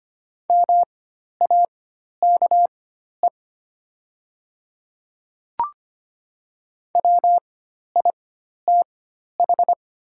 Morse code, heard as a keyed tone, about 25 wpm, spaced as for 12 wpm, 700 Hz. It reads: MAKE WITH
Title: 4x Character Spacing - Top 100 Words - 25wpm